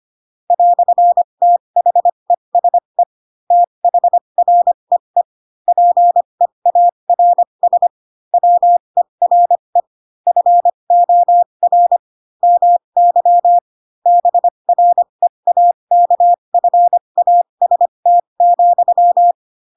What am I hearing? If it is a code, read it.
Morse code, heard as a keyed tone, about 25 wpm, spaced as for 22 wpm, 700 Hz. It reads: "THESE THREE PEARS WERE FOR MY BREAKFAST,
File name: Pinocchio / Pinocchio ch7 18 25 ik1hge